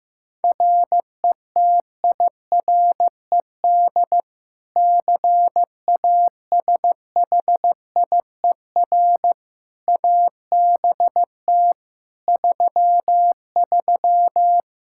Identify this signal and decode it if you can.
Morse code, heard as a keyed tone, about 15 wpm, 700 Hz. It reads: RETIRED CASHIER ABT 33
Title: Practice QSO - Exchange 3 - More Info - 15wpm